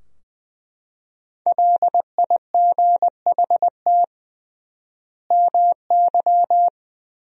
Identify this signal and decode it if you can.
Morse code, heard as a keyed tone, about 20 wpm, 700 Hz. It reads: LIGHT MY
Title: 3x Word-Spacing - Sentences from Top 400 Words - 20wpm